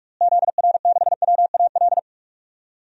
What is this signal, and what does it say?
Morse code, heard as a keyed tone, about 46 wpm, 700 Hz. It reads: ZR6WAL
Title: DXCC-100 Call Signs - 6 Characters - Repeated 3x - 45wpm